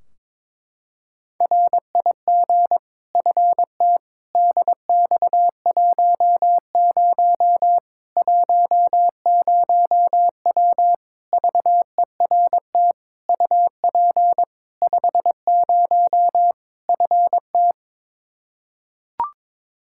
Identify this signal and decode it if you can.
Morse code, heard as a keyed tone, about 22 wpm, 700 Hz. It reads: RIG FT DX10 10W VERT UP 50 FT
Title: Practice QSO - Exchange 2 - Info - 22wpm